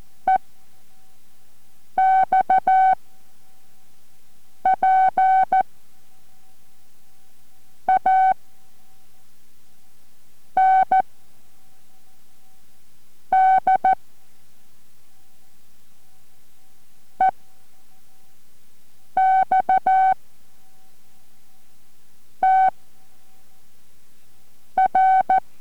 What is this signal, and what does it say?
Morse code, heard as a keyed tone, about 14 wpm, 760 Hz. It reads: E X P A N D E X T R